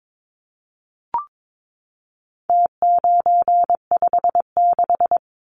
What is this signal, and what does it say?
Morse code, heard as a keyed tone, about 22 wpm, 700 Hz. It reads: T956
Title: Numbers - Contest Style - 4-Digit 22wpm